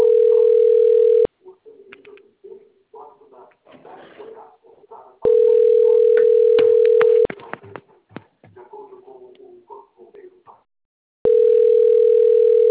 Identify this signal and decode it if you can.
Ringback tone (North America), 440 + 480 Hz.